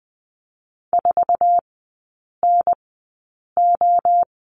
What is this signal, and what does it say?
Morse code, heard as a keyed tone, about 20 wpm, spaced as for 8 wpm, 700 Hz. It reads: 4NO
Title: 178 Lesson 29e - Slash - Three Characters 20wpm